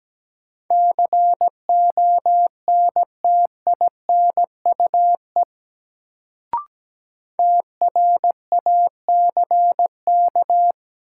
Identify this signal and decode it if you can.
Morse code, heard as a keyed tone, about 17 wpm, 700 Hz. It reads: CONTINUE TRACK
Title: Sentences from Top 1100 Words - New Word Drill Alt - 17wpm